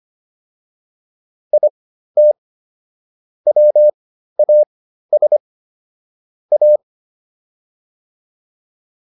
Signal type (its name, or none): Morse code